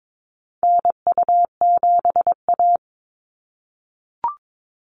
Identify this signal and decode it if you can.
Morse code, heard as a keyed tone, about 22 wpm, 700 Hz. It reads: NU7A
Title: NA SOTA Activator Callsigns 22wpm